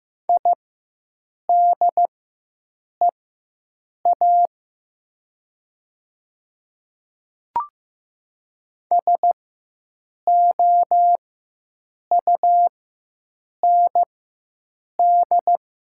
Morse code, heard as a keyed tone, about 15 wpm, spaced as for 7 wpm, 700 Hz. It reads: IDEA SOUND
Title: 4x Character Spacing - Top 300 Words - 15wpm